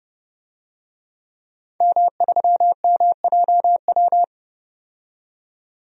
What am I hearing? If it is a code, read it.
Morse code, heard as a keyed tone, about 30 wpm, 700 Hz. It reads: M3MJW
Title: DXCC-100 Call Signs - 5 Characters - 30wpm